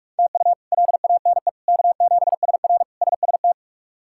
Morse code, heard as a keyed tone, about 46 wpm, 700 Hz. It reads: TU LANE K7SR SST
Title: K1USN SST QSO Practice - 45wpm